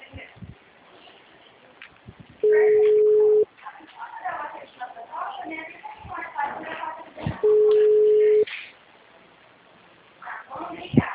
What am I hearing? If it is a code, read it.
Ringback tone (Europe, 425 Hz), with speech.
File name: out- -2001- - - .419